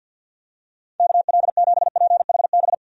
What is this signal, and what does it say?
Morse code, heard as a keyed tone, about 46 wpm, 700 Hz. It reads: KL7PHB
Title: DXCC-100 Call Signs - 6 Characters - 50wpm